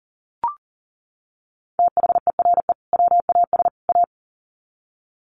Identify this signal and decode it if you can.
Morse code, heard as a keyed tone, about 40 wpm, 700 Hz. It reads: THERE WAS A
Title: Sets of 3 Words 40wpm